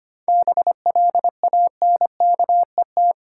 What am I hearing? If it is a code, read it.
Morse code, heard as a keyed tone, about 25 wpm, 700 Hz. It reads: BLANKET
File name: practice / 25wpm / Easy Words with Hints - 25wpm